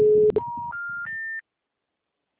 Special information tone.
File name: out- -2011- - - .15286